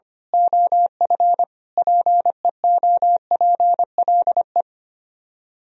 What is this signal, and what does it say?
Morse code, heard as a keyed tone, about 24 wpm, 700 Hz. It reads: OF PEOPLE
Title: Speed-Racing - Sets of 5 Words 25wpm